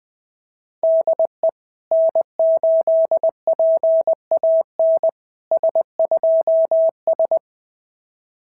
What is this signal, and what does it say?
Morse code, heard as a keyed tone, about 20 wpm, 660 Hz. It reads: DE N8PAN S2S
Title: SOTA QSO Practice 20wpm